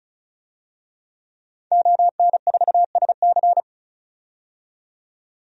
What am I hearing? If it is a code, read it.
Morse code, heard as a keyed tone, about 36 wpm, 700 Hz. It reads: ON4SC
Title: DXCC-100 Call Signs - 5 Characters - 35wpm